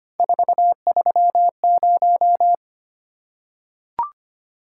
Morse code, heard as a keyed tone, about 25 wpm, 700 Hz. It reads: 430
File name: Three Letter-Number Combinations - 25wpm